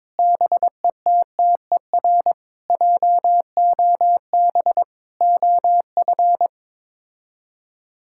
Morse code, heard as a keyed tone, about 22 wpm, 700 Hz. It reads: BETTER JOB OF
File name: practice / 22wpm / Sets of 7 Words - 22wpm